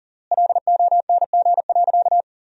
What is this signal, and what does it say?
Morse code, heard as a keyed tone, about 40 wpm, 700 Hz. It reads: LONG.
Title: 3x Word-Spacing - Sentences from Top 500 Words - 40wpm